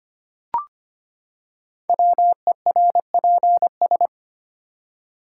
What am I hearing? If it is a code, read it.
Morse code, heard as a keyed tone, about 25 wpm, 700 Hz. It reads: WERPS
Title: Code Groups - 5 Letters - 25wpm